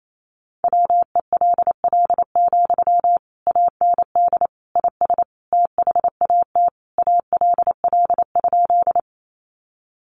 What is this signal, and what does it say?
Morse code, heard as a keyed tone, about 28 wpm, 700 Hz. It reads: WELL, AND IS THAT ALL?
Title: Sentences from Top 200 Words 28wpm